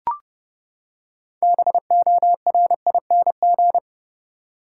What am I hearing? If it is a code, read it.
Morse code, heard as a keyed tone, about 30 wpm, 700 Hz. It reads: BORING
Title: Six Letter Words - 30wpm